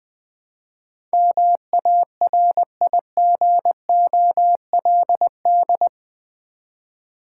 Morse code, heard as a keyed tone, about 21 wpm, 690 Hz. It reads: MARIGOLD
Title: Difficult Words with Hints - 20wpm